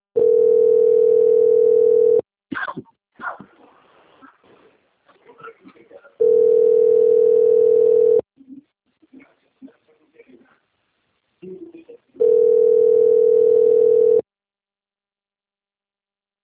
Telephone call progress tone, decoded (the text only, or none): ringback tone